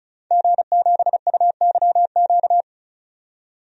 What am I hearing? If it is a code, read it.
Morse code, heard as a keyed tone, about 34 wpm, 700 Hz. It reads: G7UYQ
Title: DXCC-100 Call Signs - 5 Characters - 35wpm